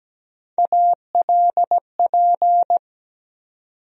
Morse code, heard as a keyed tone, about 17 wpm, 700 Hz. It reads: ALP
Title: USA Call Sign Postfixes - 3 Letters 17wpm